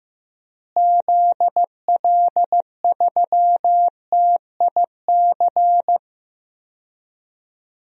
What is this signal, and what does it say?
Morse code, heard as a keyed tone, about 15 wpm, 700 Hz. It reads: ZL3TIC
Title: DXCC-100 Call Signs - 6 Characters - Repeated 2x - 15wpm